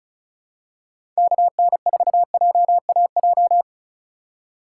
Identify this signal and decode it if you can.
Morse code, heard as a keyed tone, about 36 wpm, 700 Hz. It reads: KN4JAJ
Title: USA Call Signs - 6 Characters - 4x Character Spacing - 35wpm